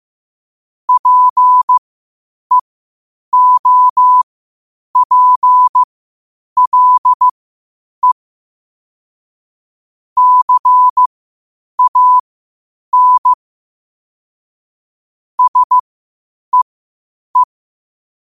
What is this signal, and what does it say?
Morse code, heard as a keyed tone, about 15 wpm, spaced as for 8 wpm, 1000 Hz. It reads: PEOPLE CAN SEE
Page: New quotes every day in morse code at 8 Words per minute.